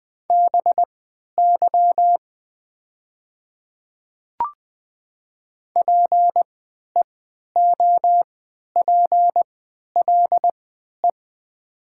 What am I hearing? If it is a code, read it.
Morse code, heard as a keyed tone, about 20 wpm, spaced as for 11 wpm, 700 Hz. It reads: BY PEOPLE
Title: 3x Character Spacing - Top 200 Words - 20wpm